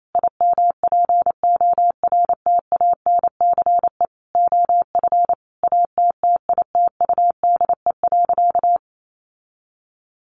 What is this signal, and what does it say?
Morse code, heard as a keyed tone, about 28 wpm, 700 Hz. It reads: IMPORTANCE OF ATTITUDE.
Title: Sentences from Top 1300 Words 28wpm